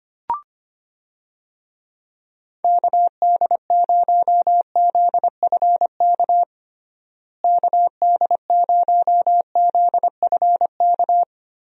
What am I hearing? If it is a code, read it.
Morse code, heard as a keyed tone, about 25 wpm, 700 Hz. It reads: KD0ZFK KD0ZFK
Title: USA Call Signs - 6 Characters - Repeated 3x - 25wpm